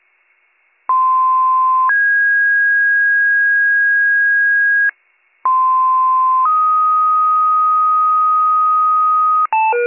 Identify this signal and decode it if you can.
Two-tone page, heard as a one-second tone, then a three-second tone, about 1040 then 1650 Hz, about 1040 then 1250 Hz, with noise. It is heard from 0.9 to 9.5 s.